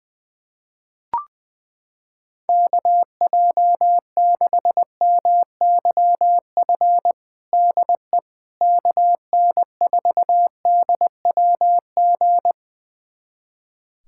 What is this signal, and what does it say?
Morse code, heard as a keyed tone, about 20 wpm, 700 Hz. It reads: KJ6MYF DE KN4DWG
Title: USA Callsign DE Callsign 20wpm